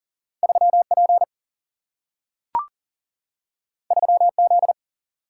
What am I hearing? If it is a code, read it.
Morse code, heard as a keyed tone, about 42 wpm, 700 Hz. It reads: 3P 3Z
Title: Single-Letter Single-Number 40wpm